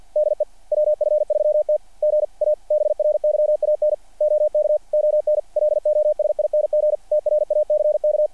Morse code, heard as a keyed tone, about 48 wpm, 600 Hz. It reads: DE WF3T GA BRYAN OK ON LOSING TRACK
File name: archived / CRSnet